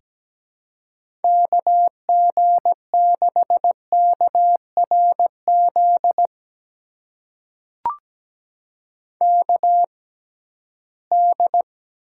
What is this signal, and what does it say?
Morse code, heard as a keyed tone, about 17 wpm, spaced as for 13 wpm, 700 Hz. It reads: KG6KRZ KD
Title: USA Call Signs - 6 Characters - 6x Character Spacing - 17wpm